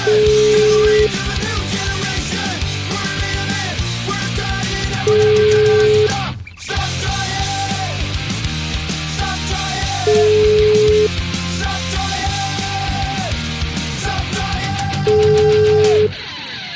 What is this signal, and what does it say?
Ringback tone (Europe, 425 Hz), with noise.